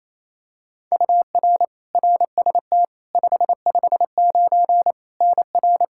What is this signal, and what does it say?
Morse code, heard as a keyed tone, about 28 wpm, 700 Hz. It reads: UR RST 559 NR
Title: K1USN SST QSO Practice 28wpm